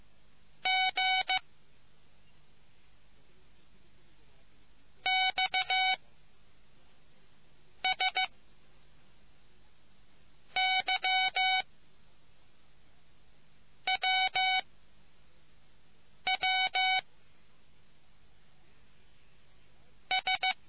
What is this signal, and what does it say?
Morse code, heard as a keyed tone, about 15 wpm, 750 Hz. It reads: G X S Y W W S